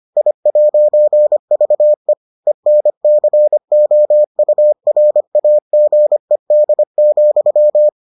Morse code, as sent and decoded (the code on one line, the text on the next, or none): .. .----. ...- . / . -. -.-. --- ..- .-. .- --. -.. --..--
I'VE ENCOURAGED,